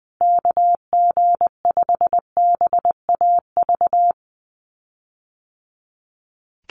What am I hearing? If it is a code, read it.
Morse code, heard as a keyed tone, about 20 wpm, 700 Hz. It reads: KG5BAV
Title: USA Call Signs Repeated 2x 20wpm